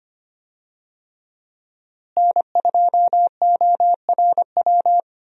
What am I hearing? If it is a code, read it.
Morse code, heard as a keyed tone, about 25 wpm, 700 Hz. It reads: N2ORW
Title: USA Call Signs - 5 Characters - Repeated 3x - 25wpm